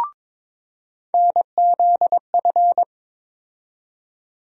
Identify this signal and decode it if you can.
Morse code, heard as a keyed tone, about 22 wpm, 700 Hz. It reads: NZF
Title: USA Call Sign Postfixes - 3 Letters 22wpm